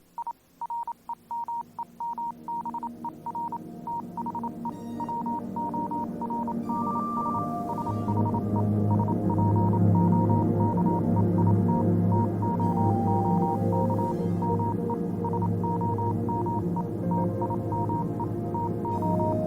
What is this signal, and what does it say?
Morse code, heard as a keyed tone, about 27 wpm, 940 Hz, with noise. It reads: I REMEMBER THE WORDS SHE SPOKE IT TOOK ME SOME TIME TO